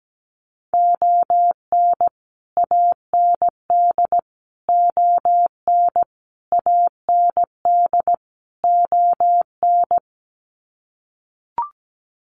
Morse code, heard as a keyed tone, about 17 wpm, 700 Hz. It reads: ON AND ON AND ON
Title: Sets of 5 Words Encore2 17wpm